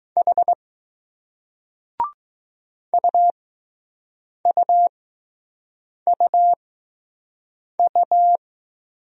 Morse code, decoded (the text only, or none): H UUUU